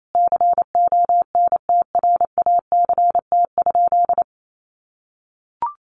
Morse code, heard as a keyed tone, about 28 wpm, 700 Hz. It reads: CONTRACT?
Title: Sentences from Top 1400 Words 28wpm